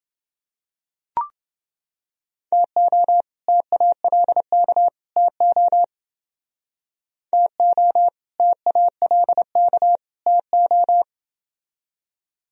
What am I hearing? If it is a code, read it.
Morse code, heard as a keyed tone, about 29 wpm, 700 Hz. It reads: TO TALK TO TO TALK TO